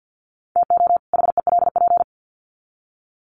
Morse code, heard as a keyed tone, about 50 wpm, 700 Hz. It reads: TO HELP
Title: Sets of 2 Words 50wpm